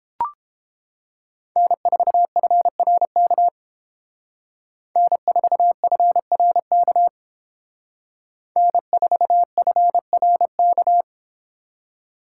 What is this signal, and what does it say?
Morse code, heard as a keyed tone, about 30 wpm, 700 Hz. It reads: N4FRK N4FRK N4FRK